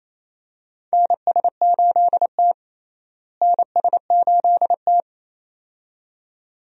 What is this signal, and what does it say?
Morse code, heard as a keyed tone, about 28 wpm, 700 Hz. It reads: NS8T NS8T
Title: USA Call Signs - 4 Characters - Repeated 2x - 28wpm